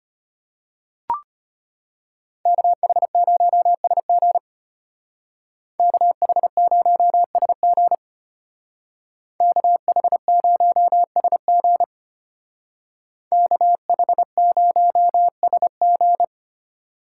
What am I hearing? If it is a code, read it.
Morse code, heard as a keyed tone, about 33 wpm, 700 Hz. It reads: KH0SG KH0SG KH0SG KH0SG